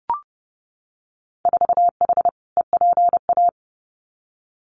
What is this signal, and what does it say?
Morse code, heard as a keyed tone, about 32 wpm, 700 Hz. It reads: 4H EPA
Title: Winter Field Day Exchanges 30wpm